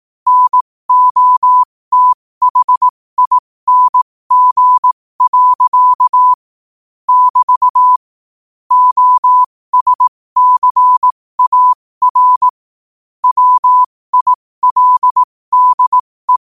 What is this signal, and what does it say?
Morse code, heard as a keyed tone, about 18 wpm, spaced as for 15 wpm, 1000 Hz. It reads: NOTHING. <BT> OSCAR WILDE
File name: qotdfiles / 15wpm